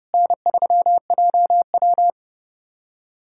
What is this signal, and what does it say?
Morse code, heard as a keyed tone, about 32 wpm, 700 Hz. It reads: N3JW
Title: USA Callsign DE Callsign - 4 Characters - 30wpm